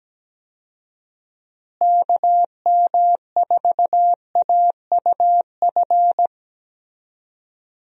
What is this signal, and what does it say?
Morse code, heard as a keyed tone, about 17 wpm, 700 Hz. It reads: KM4AUF
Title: USA Call Signs - 6 Characters - Repeated 3x - 17wpm